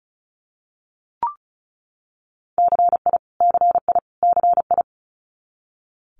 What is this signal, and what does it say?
Morse code, heard as a keyed tone, about 36 wpm, 700 Hz. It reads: CI CI CI
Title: Multi-letter Phonemes Repeated 3x 35wpm